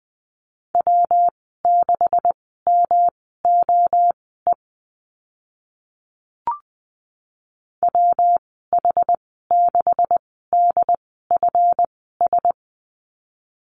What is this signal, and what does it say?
Morse code, heard as a keyed tone, about 20 wpm, spaced as for 14 wpm, 700 Hz. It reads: W6MOE WH6DFS
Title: 147 Lesson 24g - Letter y - Call Signs 20wpm